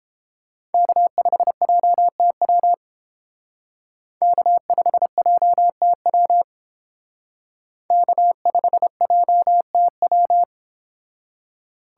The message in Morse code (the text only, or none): K5JTW K5JTW K5JTW